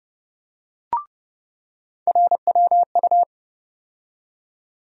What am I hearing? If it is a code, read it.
Morse code, heard as a keyed tone, about 30 wpm, 700 Hz. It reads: RWU
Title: USA Call Sign Postfixes - 3 Letters 30wpm